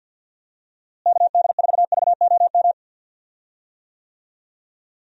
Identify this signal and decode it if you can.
Morse code, heard as a keyed tone, about 50 wpm, 700 Hz. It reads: KD4VOM
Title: USA Call Signs - 6 Characters - Repeated 3x - 50wpm